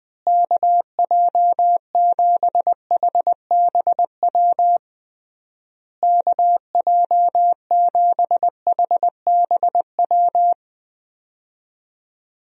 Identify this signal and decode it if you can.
Morse code, heard as a keyed tone, about 20 wpm, 700 Hz. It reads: KJ7HBW KJ7HBW
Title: USA Call Signs - 6 Characters - Repeated 3x - 20wpm